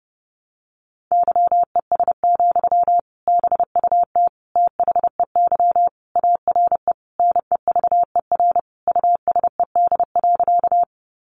Morse code, heard as a keyed tone, about 30 wpm, 700 Hz. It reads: YES, BUT THEY ARE NEVER USED.
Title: Sentences from Top 600 Words 30wpm